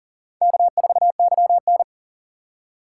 Morse code, heard as a keyed tone, about 54 wpm, spaced as for 47 wpm, 700 Hz. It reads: K4YN
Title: Speed Racing - USA Call Signs 40wpm